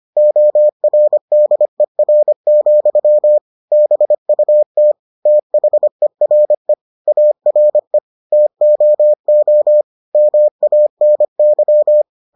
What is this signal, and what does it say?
Morse code, heard as a keyed tone, about 25 wpm, 600 Hz. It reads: ORDER, BUT THERE ARE TOO MANY